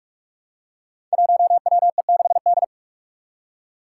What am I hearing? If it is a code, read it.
Morse code, heard as a keyed tone, about 44 wpm, 700 Hz. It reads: 1WE6D
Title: Five Challenging Letter-Number Combinations 45wpm